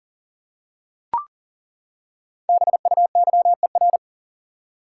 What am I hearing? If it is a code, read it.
Morse code, heard as a keyed tone, about 37 wpm, 700 Hz. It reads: BUYER
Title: Five Letter Words - 40wpm